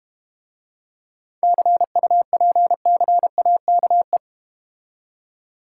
Morse code, heard as a keyed tone, about 32 wpm, 700 Hz. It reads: CUPCAKE